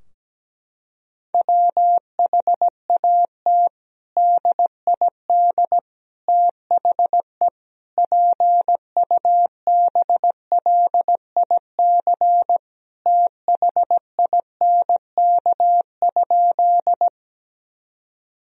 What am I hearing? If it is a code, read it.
Morse code, heard as a keyed tone, about 17 wpm, 700 Hz. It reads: WHAT DID THE PUBLIC THINK?
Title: Sentences from Top 1000 Words - Review Part 3 - 17wpm